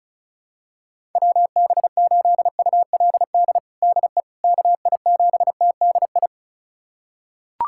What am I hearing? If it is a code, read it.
Morse code, heard as a keyed tone, about 34 wpm, 700 Hz. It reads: WB8ULD DE KI7TDI
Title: USA Callsign DE Callsign - 6 Characters - 35wpm